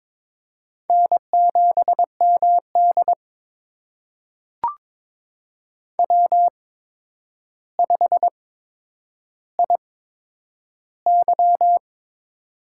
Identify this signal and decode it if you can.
Morse code, heard as a keyed tone, about 22 wpm, spaced as for 9 wpm, 700 Hz. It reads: N7MD W5IY
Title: USA Call Signs - 4 Characters - 8x Character Spacing - 22wpm